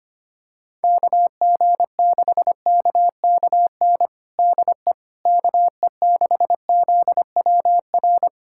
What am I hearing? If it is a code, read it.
Morse code, heard as a keyed tone, about 25 wpm, 700 Hz. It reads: KG6KKN DE KE6ZWR
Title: USA Callsign DE Callsign - 6 Characters - 25wpm